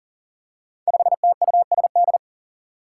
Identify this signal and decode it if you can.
Morse code, heard as a keyed tone, about 40 wpm, 700 Hz. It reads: 5TUSD
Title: Five Challenging Letter-Number Combinations 40wpm